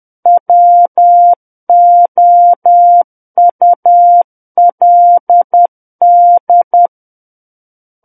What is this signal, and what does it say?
Morse code, heard as a keyed tone, about 10 wpm, 700 Hz. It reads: WOULD
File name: Book One / 01-Chapter 1 - 10 WPM